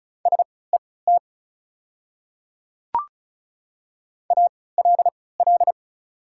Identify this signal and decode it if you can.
Morse code, heard as a keyed tone, about 35 wpm, spaced as for 20 wpm, 700 Hz. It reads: SET ALL
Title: 3x Character Spacing - Top 200 Words - 35wpm